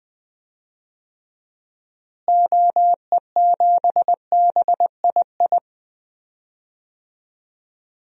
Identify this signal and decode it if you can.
Morse code, heard as a keyed tone, about 20 wpm, 700 Hz. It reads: OE7BII
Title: DXCC-100 Call Signs - 6 Characters - Repeated 3x - 20wpm